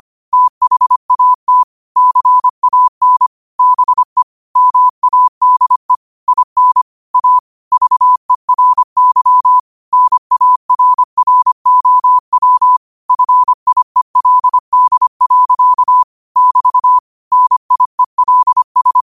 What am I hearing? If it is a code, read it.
Morse code, heard as a keyed tone, about 25 wpm, 1000 Hz. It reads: THAT CAN BE MADE IN A VERY NARROW FIELD. <BT> NIELS
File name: qotdfiles / 25wpm